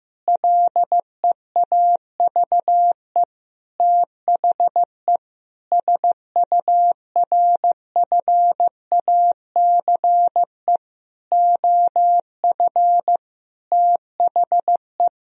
Morse code, heard as a keyed tone, about 15 wpm, 700 Hz. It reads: LEAVE THE SURFACE OF THE
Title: Sentences from Top 1000 Words - Review Part 3 - 15wpm